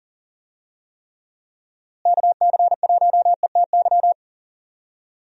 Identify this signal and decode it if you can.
Morse code, heard as a keyed tone, about 40 wpm, 700 Hz. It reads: KC1ETY
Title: USA Call Signs - 6 Characters - Repeated 3x - 40wpm